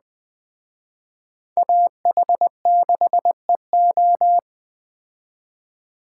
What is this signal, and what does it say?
Morse code, heard as a keyed tone, about 20 wpm, 700 Hz. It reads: AH6EO
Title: Speed Racing - USA Call Signs 17wpm